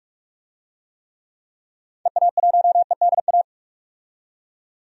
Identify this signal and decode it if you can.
Morse code, heard as a keyed tone, about 46 wpm, 700 Hz. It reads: EA1EDA
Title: DXCC-100 Call Signs - 6 Characters - 45wpm